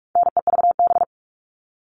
Morse code, heard as a keyed tone, about 44 wpm, 700 Hz. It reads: NEVB
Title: All Cut Numbers - 4-Digit 45wpm